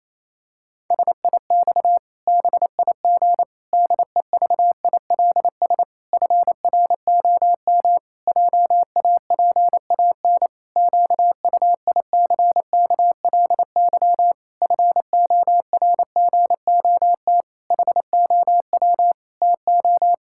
Morse code, heard as a keyed tone, about 28 wpm, 700 Hz. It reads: SIX BIG DEVILS FROM JAPAN QUICKLY FORGOT HOW TO
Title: Pangram Challenge v2 - 28wpm